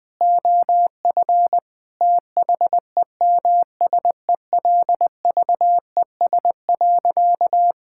Morse code, heard as a keyed tone, about 20 wpm, 700 Hz. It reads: OF THEMSELVES.
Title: Sentences from Top 1000 Words - Review Part 5 - 20wpm